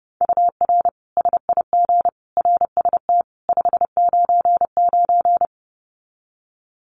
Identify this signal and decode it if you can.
Morse code, heard as a keyed tone, about 30 wpm, 700 Hz. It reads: UR SIG RST 599
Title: Common QSO Phrases 30wpm